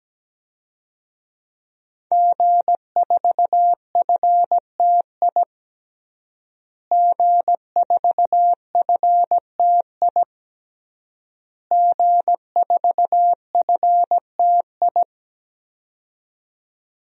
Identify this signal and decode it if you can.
Morse code, heard as a keyed tone, about 17 wpm, 700 Hz. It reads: G4FTI G4FTI G4FTI
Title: DXCC-100 Call Signs - 5 Characters - Repeated 3x - 17wpm